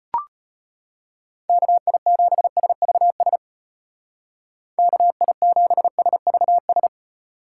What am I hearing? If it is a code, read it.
Morse code, heard as a keyed tone, about 32 wpm, 700 Hz. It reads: KI7SVS KI7SVS